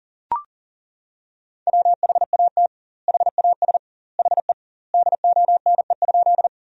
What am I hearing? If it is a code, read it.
Morse code, heard as a keyed tone, about 40 wpm, 700 Hz. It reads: WHAT HAS HE DONE?
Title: 2x Word-Spacing - Sentences from Top 500 Words - 40wpm